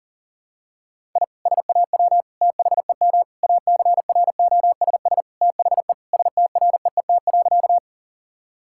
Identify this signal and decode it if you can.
Morse code, heard as a keyed tone, about 40 wpm, 700 Hz. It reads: I SAW THEM ACROSS THE STREET.
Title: Sentences from Top 1000 Words - Review Part 5 - 40wpm